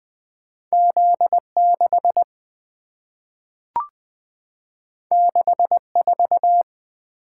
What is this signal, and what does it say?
Morse code, heard as a keyed tone, about 20 wpm, 700 Hz. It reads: Z6 64
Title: Two Letter-Number Combinations - 20wpm